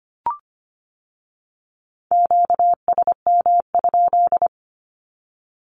Morse code, heard as a keyed tone, about 25 wpm, 700 Hz. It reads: QSM?
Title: Q-Codes with Spoken Meaning - 25wpm